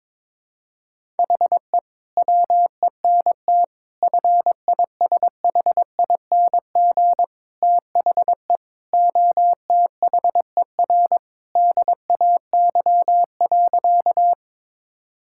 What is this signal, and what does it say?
Morse code, heard as a keyed tone, about 22 wpm, 700 Hz. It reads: HE WENT FISHING THE OTHER DAY.
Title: Sentences from Top 1000 Words - Review Part 5 - 22wpm